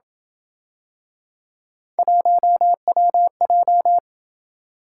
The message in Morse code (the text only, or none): 1WJ